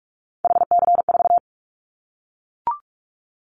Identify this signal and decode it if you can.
Morse code, heard as a keyed tone, about 44 wpm, 700 Hz. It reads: HC4
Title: Callsign Prefixes for Top 100 DXCC - 45wpm